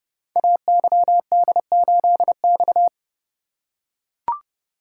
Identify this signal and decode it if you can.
Morse code, heard as a keyed tone, about 29 wpm, 700 Hz. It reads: AYD8X
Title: Code Groups - 5 Letter:Numbers - 30wpm